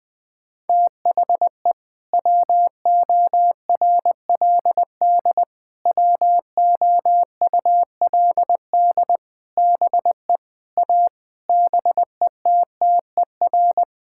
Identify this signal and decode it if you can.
Morse code, heard as a keyed tone, about 20 wpm, 700 Hz. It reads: THE WORLD WOULD BE A BETTER
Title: Sentences with 8-Word Phrases - Spoken Beginning - 20wpm